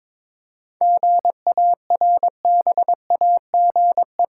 Morse code, heard as a keyed tone, about 21 wpm, 700 Hz. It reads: GARBAGE
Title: Seven Letter Words - 22wpm